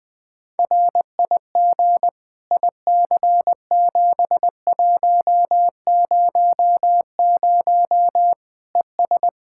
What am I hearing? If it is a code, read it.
Morse code, heard as a keyed tone, about 20 wpm, 700 Hz. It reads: RIG IC7100 ES
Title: K1USN SST QSO Practice 20wpm